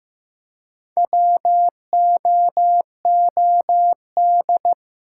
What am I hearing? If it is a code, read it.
Morse code, heard as a keyed tone, about 15 wpm, 700 Hz. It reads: WOOD
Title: Four Letter Words - 15wpm